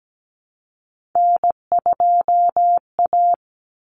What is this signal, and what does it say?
Morse code, heard as a keyed tone, about 17 wpm, 700 Hz. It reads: N2A
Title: Common Cut Numbers - 3-Digit 17wpm